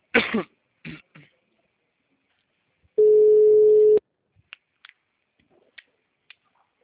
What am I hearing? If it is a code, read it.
Ringback tone (Europe, 425 Hz).
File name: out- -2009